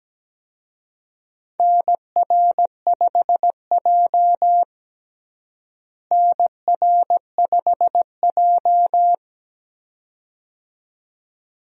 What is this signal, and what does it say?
Morse code, heard as a keyed tone, about 17 wpm, 700 Hz. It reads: NR5J NR5J
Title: USA Call Signs - 4 Characters - Repeated 2x - 17wpm